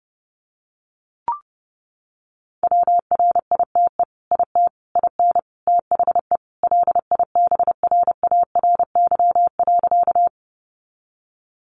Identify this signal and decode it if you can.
Morse code, heard as a keyed tone, about 28 wpm, 700 Hz. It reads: WRITE IT IN THE LIBRARY.
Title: Sentences from Top 600 Words 30wpm